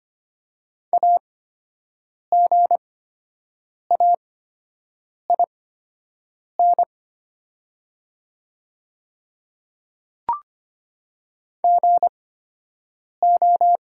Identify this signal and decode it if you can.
Morse code, heard as a keyed tone, about 25 wpm, spaced as for 7 wpm, 700 Hz. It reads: AGAIN GO
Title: 8x Character Spacing - Top 300 Words - 25wpm